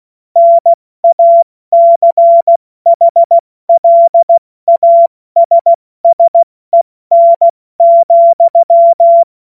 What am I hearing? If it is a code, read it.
Morse code, heard as a keyed tone, about 16 wpm, 680 Hz. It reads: NACHLASSEN,
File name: Morsekurs / Witze / Witz 11. Januar 2022, 70 F80